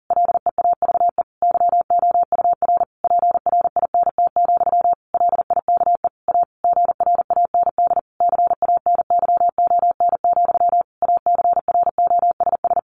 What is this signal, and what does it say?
Morse code, heard as a keyed tone, about 40 wpm, 700 Hz. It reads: LEAVE YOUR PRINT, LIKE A GRAND CANYON, ACROSS
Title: A Sound Of Thunder 40wpm